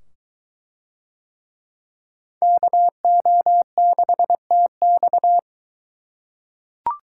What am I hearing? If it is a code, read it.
Morse code, heard as a keyed tone, about 22 wpm, 700 Hz. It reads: KO6TX